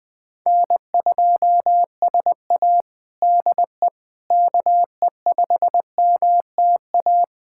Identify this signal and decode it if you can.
Morse code, heard as a keyed tone, about 20 wpm, 700 Hz. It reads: N2SA DE KE5MTA
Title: USA Callsign DE Callsign 20wpm